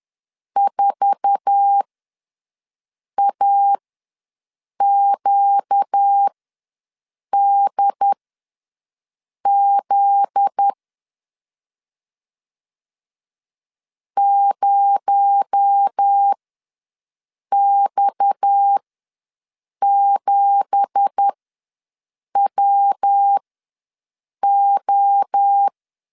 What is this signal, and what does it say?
Morse code, heard as a keyed tone, about 10 wpm, spaced as for 6 wpm, 800 Hz. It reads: 4AQDZ 0X7WO
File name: LEZIONI / 10 SIGNS / 10B INT ws 10wpm 4134